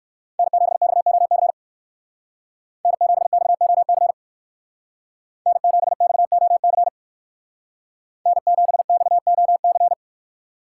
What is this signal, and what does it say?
Morse code, heard as a keyed tone, about 58 wpm, 700 Hz. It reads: N7XOC N7XOC N7XOC N7XOC